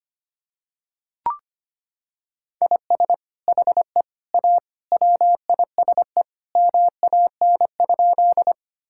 Morse code, heard as a keyed tone, about 25 wpm, 700 Hz. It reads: IS HE A WISE MAN?
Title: Sentences from Top 1600 Words 25wpm